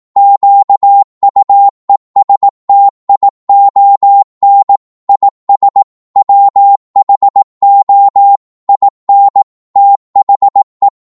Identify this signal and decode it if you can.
Morse code, heard as a keyed tone, about 18 wpm, 800 Hz. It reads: QUESTION IS WHO IN THE
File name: Morse / psg-18wpm